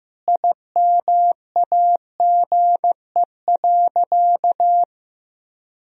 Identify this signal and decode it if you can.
Morse code, heard as a keyed tone, about 15 wpm, 700 Hz. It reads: IMAGE.
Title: Sentences from Top 1000 Words - Review Part 4 - 15wpm